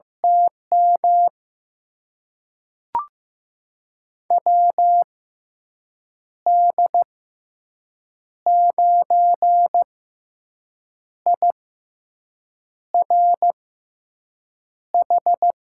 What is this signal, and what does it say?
Morse code, heard as a keyed tone, about 15 wpm, spaced as for 6 wpm, 700 Hz. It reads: TM WD9IRH
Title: USA Call Signs - 6 Characters - 6x Character Spacing - 15wpm